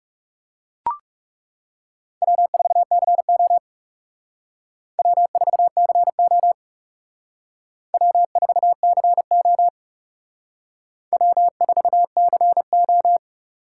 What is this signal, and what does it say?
Morse code, heard as a keyed tone, about 37 wpm, 700 Hz. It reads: W4CO W4CO W4CO W4CO
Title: Speed Racing - USA 1x2 Call Signs 30wpm